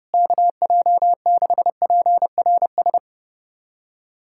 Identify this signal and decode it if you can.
Morse code, heard as a keyed tone, about 30 wpm, 700 Hz. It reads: KJ6PRS
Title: USA Callsign DE Callsign 30wpm